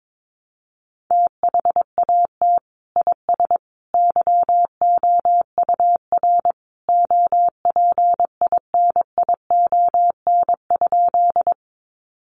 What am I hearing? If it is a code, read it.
Morse code, heard as a keyed tone, about 22 wpm, 700 Hz. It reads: THAT IS YOUR OPINION?
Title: Sentences from Top 1200 Words 22wpm